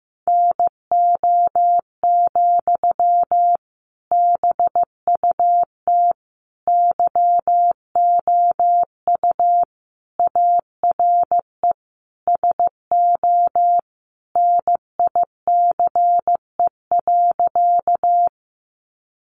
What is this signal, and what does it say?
Morse code, heard as a keyed tone, about 15 wpm, 700 Hz. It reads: NO, BUT YOU ARE SO NICE.
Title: Sentences from Top 1100 Words 15wpm